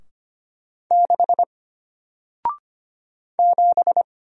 Morse code, heard as a keyed tone, about 25 wpm, 700 Hz. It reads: 6 7
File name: Single-Letter-Number-25wpm